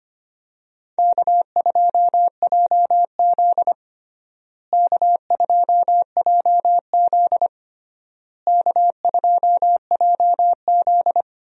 Morse code, heard as a keyed tone, about 25 wpm, 700 Hz. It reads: K2JZ K2JZ K2JZ
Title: USA Call Signs - 4 Characters - Repeated 3x - 25wpm